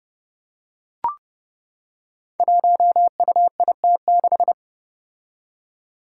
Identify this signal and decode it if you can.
Morse code, heard as a keyed tone, about 32 wpm, 700 Hz. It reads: 1UIT6
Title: Five Challenging Letter-Number Combinations 30wpm